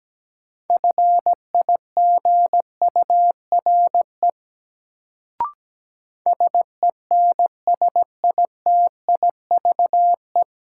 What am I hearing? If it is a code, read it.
Morse code, heard as a keyed tone, about 17 wpm, 700 Hz. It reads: FIGURE SENSITIVE
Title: Sentences from Top 1400 Words - New Word Drill Alt - 17wpm